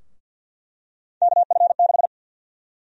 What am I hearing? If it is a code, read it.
Morse code, heard as a keyed tone, about 50 wpm, 700 Hz. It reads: KF6
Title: USA Call Sign Prefixes - 50wpm